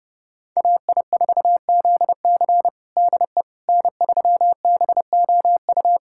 Morse code, heard as a keyed tone, about 30 wpm, 700 Hz. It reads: AI4ZC DE N3BOU
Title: USA Callsign DE Callsign - 5 Characters - 30wpm